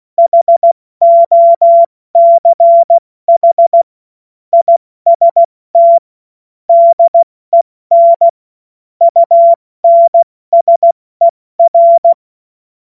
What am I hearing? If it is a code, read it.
Morse code, heard as a keyed tone, about 16 wpm, 680 Hz. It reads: HOCH IST DEN UNSER
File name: Morsekurs / Witze / Witz 14. September 2021, 70 F80